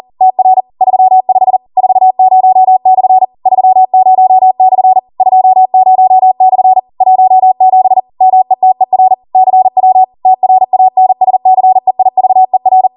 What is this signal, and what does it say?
Morse code, heard as a keyed tone, about 40 wpm, 750 Hz. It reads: TR 35 40/ 30/ 20/ 17 METER CW TRANSCEIVER
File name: Morse / 40WPM